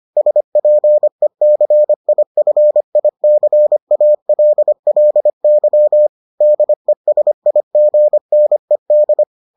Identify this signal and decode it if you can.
Morse code, heard as a keyed tone, about 25 wpm, 600 Hz. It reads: SPECIFICALLY DESIGNED